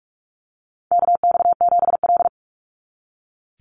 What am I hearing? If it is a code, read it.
Morse code, heard as a keyed tone, about 44 wpm, 700 Hz. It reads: KX7L
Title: NA SOTA Chaser Callsigns 45wpm